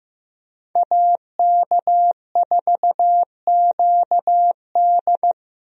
Morse code, heard as a keyed tone, about 15 wpm, 700 Hz. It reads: AK4QD
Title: USA Call Signs - 5 Characters - 15wpm